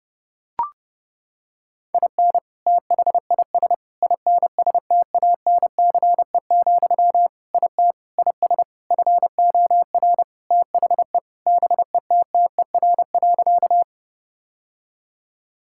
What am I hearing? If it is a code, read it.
Morse code, heard as a keyed tone, about 30 wpm, 700 Hz. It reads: IN THIS INSTANCE, IT IS FOR THE BETTER.
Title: Sentences from Top 1000 Words - Review Part 5 - 30wpm